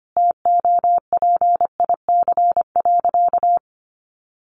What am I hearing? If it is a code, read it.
Morse code, heard as a keyed tone, about 25 wpm, 700 Hz. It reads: TOPIC.
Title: Sentences from Top 1200 Words 25wpm